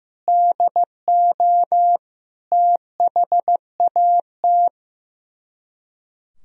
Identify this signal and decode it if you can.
Morse code, heard as a keyed tone, about 15 wpm, 700 Hz. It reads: DO THAT
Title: Speed-Racing - Sets of 5 Words 15wpm